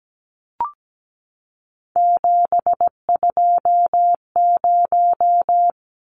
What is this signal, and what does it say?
Morse code, heard as a keyed tone, about 17 wpm, 700 Hz. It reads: 720
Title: Numbers - 3-Digit 17wpm